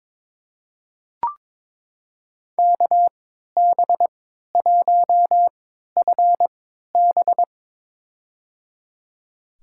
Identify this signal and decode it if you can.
Morse code, heard as a keyed tone, about 23 wpm, spaced as for 13 wpm, 700 Hz. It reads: KB1FB
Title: USA Call Signs - 5 Characters - 3x Character Spacing - 22wpm